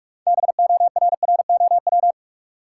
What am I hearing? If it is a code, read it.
Morse code, heard as a keyed tone, about 43 wpm, 700 Hz. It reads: BORROW
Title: Sentences from Top 1500 Words - New Word Drill - 45wpm